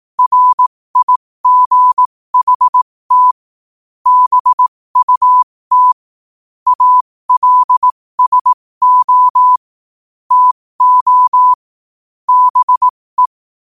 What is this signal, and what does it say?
Morse code, heard as a keyed tone, about 18 wpm, spaced as for 15 wpm, 1000 Hz. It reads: RIGHT BUT ALSO TO BE
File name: qotdfiles / 15wpm